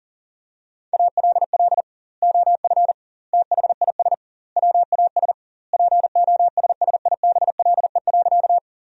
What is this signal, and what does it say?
Morse code, heard as a keyed tone, about 40 wpm, 700 Hz. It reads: ALL OF THIS WAS POSSIBLE.
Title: 2x Word-Spacing - Sentences from Top 500 Words - 40wpm